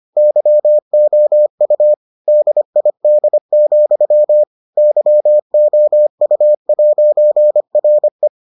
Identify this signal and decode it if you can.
Morse code, heard as a keyed tone, about 25 wpm, 600 Hz. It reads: YOU DID, YOU'RE